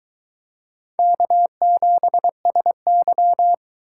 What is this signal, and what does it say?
Morse code, heard as a keyed tone, about 22 wpm, 700 Hz. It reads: K7SY